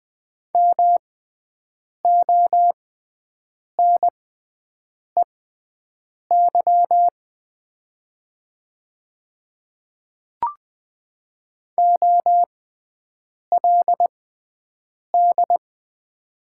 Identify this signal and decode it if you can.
Morse code, heard as a keyed tone, about 20 wpm, spaced as for 7 wpm, 700 Hz. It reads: MONEY OLD
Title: 6x Character Spacing - Top 500 Words - 20wpm